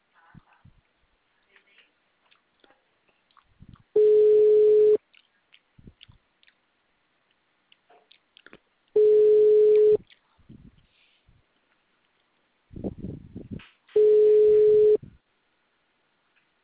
Ringback tone (Europe, 425 Hz).